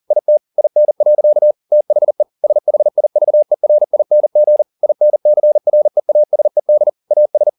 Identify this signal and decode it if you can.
Morse code, heard as a keyed tone, about 40 wpm, 600 Hz. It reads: IT IN. THE SHIVERING INCREASED AS